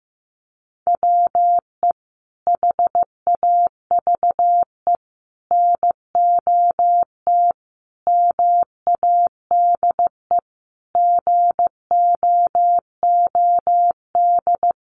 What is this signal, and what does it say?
Morse code, heard as a keyed tone, about 15 wpm, 700 Hz. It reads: WE HAVE NOT MADE GOOD
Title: Sentences from Top 1400 Words 15wpm